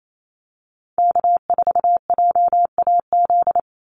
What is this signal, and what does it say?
Morse code, heard as a keyed tone, about 28 wpm, 700 Hz. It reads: K4JAZ
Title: Call Signs Repeated 3x - 28wpm